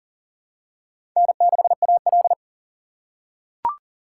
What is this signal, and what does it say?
Morse code, heard as a keyed tone, about 37 wpm, 700 Hz. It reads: N6AL
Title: USA Call Signs - 2x Character Spacing - 40wpm